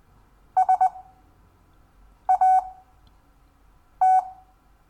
Morse code, heard as a keyed tone, about 20 wpm, spaced as for 6 wpm, 750 Hz. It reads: SAT
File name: cw / Lesson4 REVIEW